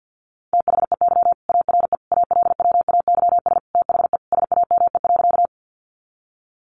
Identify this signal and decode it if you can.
Morse code, heard as a keyed tone, about 50 wpm, 700 Hz. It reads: THEY ARE ALWAYS THE SAME.
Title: Sentences from Top 300 Words 50wpm